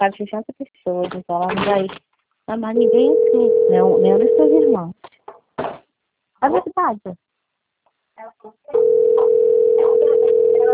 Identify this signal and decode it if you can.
Ringback tone (North America), 440 + 480 Hz, with speech.
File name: out- - - - - .65769